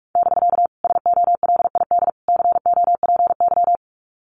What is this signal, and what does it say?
Morse code, heard as a keyed tone, about 44 wpm, 700 Hz. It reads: <BK> SOLID COPY
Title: Common QSO Phrases Encore 45wpm